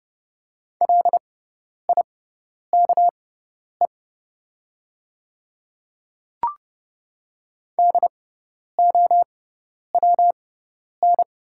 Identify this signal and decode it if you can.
Morse code, heard as a keyed tone, about 30 wpm, spaced as for 10 wpm, 700 Hz. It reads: LIKE DOWN
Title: 6x Character Spacing - Top 200 Words - 30wpm